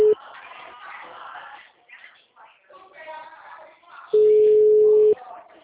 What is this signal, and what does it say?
Ringback tone (Europe, 425 Hz).